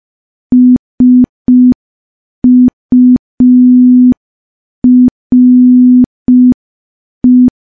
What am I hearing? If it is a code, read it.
Morse code, heard as a keyed tone, about 5 wpm, 250 Hz. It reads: SURE
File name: morse / morse-training-05WPM